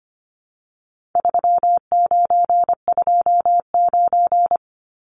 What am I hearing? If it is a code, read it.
Morse code, heard as a keyed tone, about 25 wpm, 700 Hz. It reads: 3929
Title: Numbers - 4-Digit 25wpm